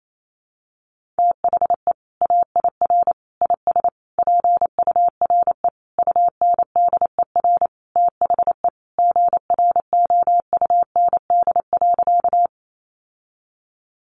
Morse code, heard as a keyed tone, about 28 wpm, 700 Hz. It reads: THE AIR IS PURE UNDER THE GROUND.
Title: Sentences from Top 1500 Words 28wpm